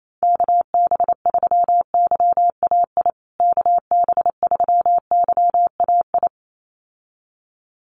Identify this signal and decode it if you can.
Morse code, heard as a keyed tone, about 28 wpm, 700 Hz. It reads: KB3YAI KB3YAI
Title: USA Call Signs Repeated 2x 28wpm